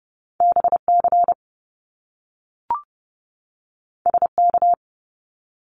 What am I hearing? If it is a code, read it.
Morse code, heard as a keyed tone, about 30 wpm, 700 Hz. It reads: BC SK
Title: Canadian Provinces and Territories 30wpm